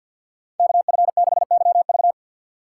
Morse code, heard as a keyed tone, about 48 wpm, 700 Hz. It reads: KF6YV
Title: USA Call Signs - 5 Characters - 2x Character Spacing - 50wpm